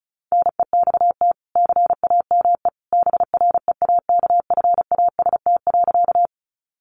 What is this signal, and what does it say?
Morse code, heard as a keyed tone, about 34 wpm, 700 Hz. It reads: NEXT CAME BREAKFAST.
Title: Sentences from Top 1500 Words 35wpm